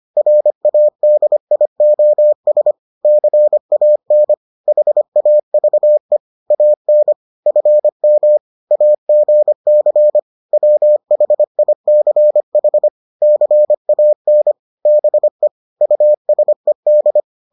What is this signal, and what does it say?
Morse code, heard as a keyed tone, about 25 wpm, 600 Hz. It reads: RADIOS CAN HAVE AN FM AGC WHICH CAN BE USED